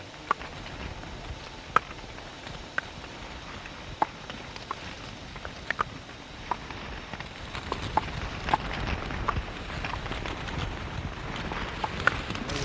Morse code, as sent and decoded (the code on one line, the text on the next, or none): none